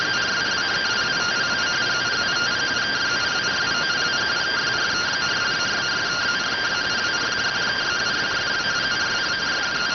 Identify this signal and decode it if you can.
Two-tone teleprinter signal, about 170 Hz shift.